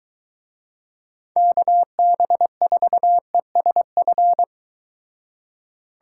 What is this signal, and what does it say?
Morse code, heard as a keyed tone, about 23 wpm, 700 Hz. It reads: KB4ESF